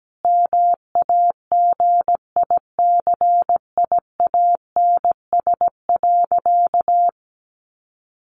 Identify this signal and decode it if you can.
Morse code, heard as a keyed tone, about 17 wpm, 700 Hz. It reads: MAGICIANS.
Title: Easy Phrases 17wpm